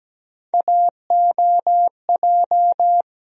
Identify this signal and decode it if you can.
Morse code, heard as a keyed tone, about 17 wpm, 700 Hz. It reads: AOJ
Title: USA Call Sign Postfixes - 3 Letters 17wpm